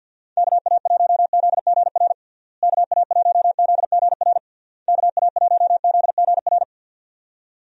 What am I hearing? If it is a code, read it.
Morse code, heard as a keyed tone, about 52 wpm, 700 Hz. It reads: KA1ZGR KA1ZGR KA1ZGR
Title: USA Call Signs - 6 Characters - Repeated 3x - 50wpm